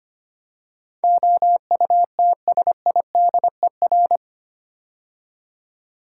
Morse code, heard as a keyed tone, about 25 wpm, 700 Hz. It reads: OUTSIDER
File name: practice / 25wpm / 8 Letter Words - First 3 Letters Given - 25wpm